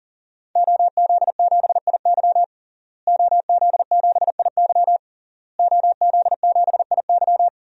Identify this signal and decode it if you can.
Morse code, heard as a keyed tone, about 40 wpm, 700 Hz. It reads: OZ7IY OZ7IY OZ7IY
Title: DXCC-100 Call Signs - 5 Characters - Repeated 3x - 40wpm